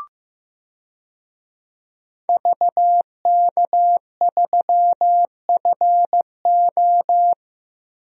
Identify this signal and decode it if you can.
Morse code, heard as a keyed tone, about 15 wpm, 700 Hz. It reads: VK3FO
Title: DXCC-100 Call Signs - 5 Characters - 15wpm